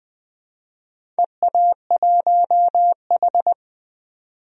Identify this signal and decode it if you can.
Morse code, heard as a keyed tone, about 20 wpm, 700 Hz. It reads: EA1H
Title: DXCC-100 Call Signs - 4 Characters - 20wpm